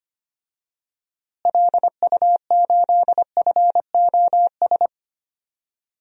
Morse code, heard as a keyed tone, about 25 wpm, 700 Hz. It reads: LU8FOS
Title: DXCC-100 Call Signs - 6 Characters - Repeated 3x - 25wpm